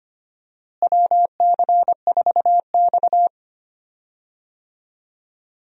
Morse code, heard as a keyed tone, about 25 wpm, 700 Hz. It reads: WC4X
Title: USA Call Signs - 4 Characters - Repeated 2x - 25wpm